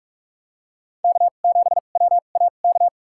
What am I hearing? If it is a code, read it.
Morse code, heard as a keyed tone, about 43 wpm, spaced as for 32 wpm, 700 Hz. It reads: K7WAK
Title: USA Call Signs - 2x Character Spacing - 45wpm